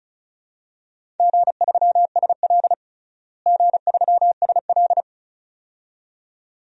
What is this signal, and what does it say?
Morse code, heard as a keyed tone, about 36 wpm, 700 Hz. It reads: G3SL G3SL
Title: DXCC-100 Call Signs - 4 Characters - Repeated 2x - 35wpm